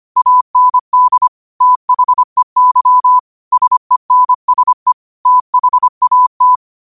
Morse code, heard as a keyed tone, about 25 wpm, 1000 Hz. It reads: AND THEY SENSE THAT